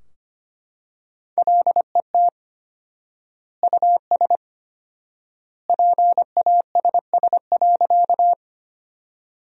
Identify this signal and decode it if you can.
Morse code, heard as a keyed tone, about 25 wpm, 700 Hz. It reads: LET US PASS.
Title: 4x Word-Spacing - Sentences from Top 400 Words - 25wpm